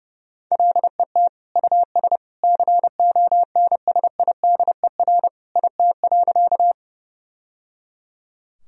Morse code, heard as a keyed tone, about 32 wpm, 700 Hz. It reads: LET US CONSIDER IT.
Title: Sentences from Top 1000 Words - Review Part 4 - 30wpm